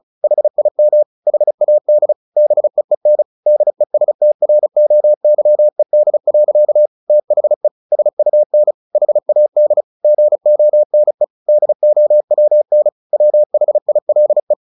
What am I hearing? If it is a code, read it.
Morse code, heard as a keyed tone, about 34 wpm, 600 Hz. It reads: HIM HAD BEEN DESTROYED. THE SUN HAD GONE DOWN WHILE
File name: the old man and the sea 2 35wpm